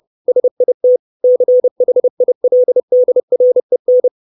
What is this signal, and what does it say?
Morse code, heard as a keyed tone, about 30 wpm, 500 Hz. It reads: SIT CHILDREN
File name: 632 Words 30WPM